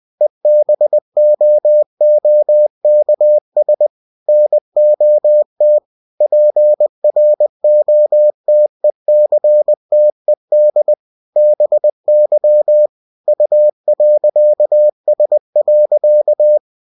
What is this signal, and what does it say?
Morse code, heard as a keyed tone, about 20 wpm, 600 Hz. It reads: EBOOKS NOT PROTECTED BY U.S.
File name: cw-books / winnie-the-pooh / winnie-20wpm-page-0112